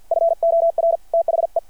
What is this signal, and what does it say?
Morse code, heard as a keyed tone, about 53 wpm, 660 Hz. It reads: FOR THE
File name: archived / CRSnet